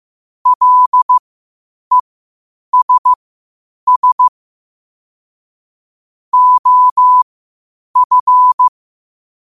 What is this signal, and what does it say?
Morse code, heard as a keyed tone, about 15 wpm, spaced as for 8 wpm, 1000 Hz. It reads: LESS OF